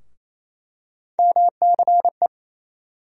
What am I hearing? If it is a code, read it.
Morse code, heard as a keyed tone, about 28 wpm, 700 Hz. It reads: MCE
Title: USA Call Sign Postfixes - 3 Letters 28wpm